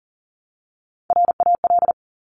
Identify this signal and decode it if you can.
Morse code, heard as a keyed tone, about 38 wpm, 700 Hz. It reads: RAL
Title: Top Three Letter Combinations in Words 40wpm